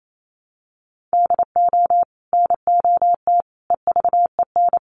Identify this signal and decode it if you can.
Morse code, heard as a keyed tone, about 28 wpm, 700 Hz. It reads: DO NOT EVEN
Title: Sets of 4 Words 28wpm